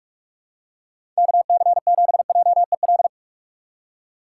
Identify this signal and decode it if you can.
Morse code, heard as a keyed tone, about 46 wpm, 700 Hz. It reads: KC7JEL
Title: USA Call Signs - 6 Characters - 2x Character Spacing - 45wpm